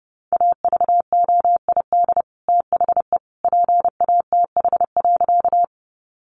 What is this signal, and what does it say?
Morse code, heard as a keyed tone, about 30 wpm, 700 Hz. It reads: AVOID THE PATH.
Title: Sentences from Top 1400 Words 30wpm